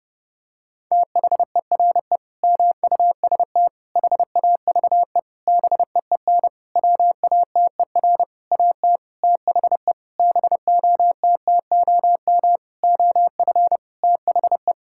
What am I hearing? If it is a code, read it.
Morse code, heard as a keyed tone, about 30 wpm, 700 Hz. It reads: THERE MUST HAVE BEEN WATER AT THE BOTTOM OF THE
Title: Sentences from Top 1000 Words - Review Part 6 - 30wpm